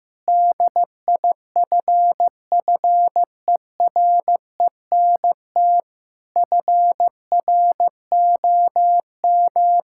Morse code, heard as a keyed tone, about 16 wpm, 700 Hz. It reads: DIFFERENT FROM
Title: Sentences from Top 1000 Words - Review Part 5 - 15wpm